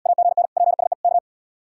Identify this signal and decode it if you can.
Morse code, heard as a keyed tone, about 71 wpm, 700 Hz. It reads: AGN PSE K